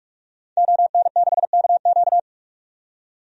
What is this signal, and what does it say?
Morse code, heard as a keyed tone, about 44 wpm, 690 Hz. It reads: ON6KQ
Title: DXCC-100 Call Signs - 5 Characters - 45wpm